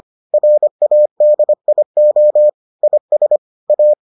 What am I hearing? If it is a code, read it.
Morse code, heard as a keyed tone, about 25 wpm, 600 Hz. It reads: RADIO IS A